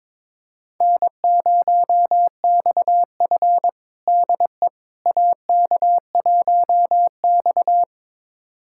Morse code, heard as a keyed tone, about 22 wpm, 700 Hz. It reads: N0XF DE AK1X
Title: USA Callsign DE Callsign - 4 Characters - 22wpm